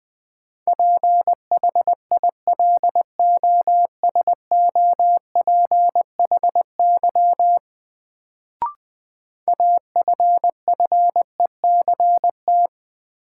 Morse code, heard as a keyed tone, about 21 wpm, 700 Hz. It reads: PHILOSOPHY AFFECT
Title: Sentences from Top 900 Words - New Word Drill - 20wpm